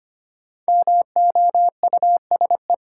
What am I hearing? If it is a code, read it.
Morse code, heard as a keyed tone, about 25 wpm, 700 Hz. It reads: MOUSE
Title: Sentences from Top 700 Words - New Word Drill - 25wpm